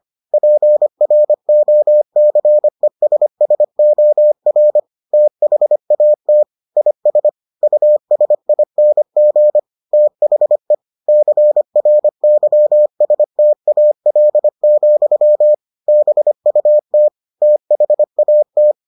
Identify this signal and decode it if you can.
Morse code, heard as a keyed tone, about 25 wpm, 600 Hz. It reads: PROCESSOR THAT IS USING THE CRYSTAL, BUT THAT